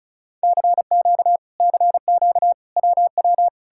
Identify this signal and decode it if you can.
Morse code, heard as a keyed tone, about 47 wpm, spaced as for 41 wpm, 700 Hz. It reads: CQ CQ WW
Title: Speed-Racing Common QSO Phrases - 35wpm